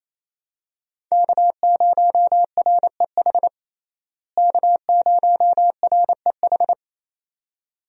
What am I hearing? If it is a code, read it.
Morse code, heard as a keyed tone, about 28 wpm, 700 Hz. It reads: K0REH K0REH
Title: USA Call Signs - 5 Characters - Repeated 2x - 28wpm